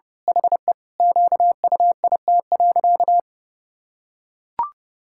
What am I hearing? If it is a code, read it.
Morse code, heard as a keyed tone, about 30 wpm, 700 Hz. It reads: HE QUIT.
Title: Pangram Challenge v2 - 30wpm